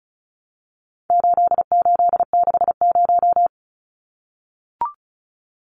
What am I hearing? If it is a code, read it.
Morse code, heard as a keyed tone, about 34 wpm, 700 Hz. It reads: 8860
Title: Numbers - 4-Digit 35wpm